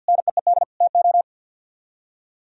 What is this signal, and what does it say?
Morse code, heard as a keyed tone, about 64 wpm, 700 Hz. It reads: NEED TO